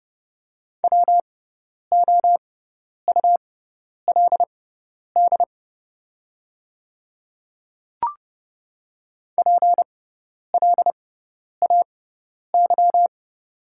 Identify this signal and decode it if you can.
Morse code, heard as a keyed tone, about 29 wpm, spaced as for 10 wpm, 700 Hz. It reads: WOULD PLAY
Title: 6x Character Spacing - Top 500 Words - 30wpm